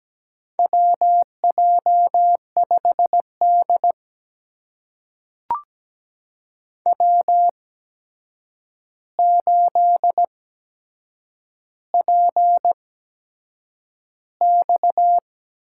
Morse code, heard as a keyed tone, about 17 wpm, spaced as for 7 wpm, 700 Hz. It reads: WJ5D W8PX
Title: USA Call Signs - 4 Characters - 8x Character Spacing - 17wpm